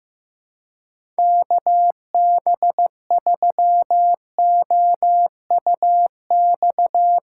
Morse code, heard as a keyed tone, about 15 wpm, 700 Hz. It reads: KB3OUX
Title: USA Callsign DE Callsign - 6 Characters - 15wpm